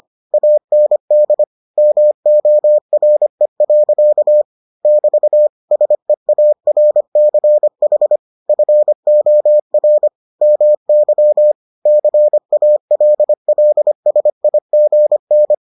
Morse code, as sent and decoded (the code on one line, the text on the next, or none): .- -. -.. / -- --- .-. .-.-.- / -...- / ... .- .-. -.-. .... / ..-. --- .-. / -- -.-- / -.-. .- .-.. .-.. ... .. --. -.
AND MORE. <BT> SEARCH FOR MY CALLSIGN